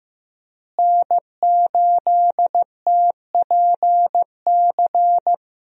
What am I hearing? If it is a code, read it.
Morse code, heard as a keyed tone, about 15 wpm, 700 Hz. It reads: N8TPC
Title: USA Call Signs - 5 Characters - Repeated 2x - 15wpm